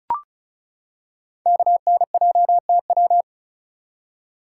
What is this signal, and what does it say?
Morse code, heard as a keyed tone, about 36 wpm, 700 Hz. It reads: KNJTW
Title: Code Groups - 5 Letter:Numbers - 35wpm